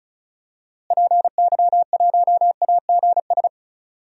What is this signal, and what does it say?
Morse code, heard as a keyed tone, about 36 wpm, 700 Hz. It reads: PY1AGS
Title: DXCC-100 Call Signs - 6 Characters - Repeated 2x - 35wpm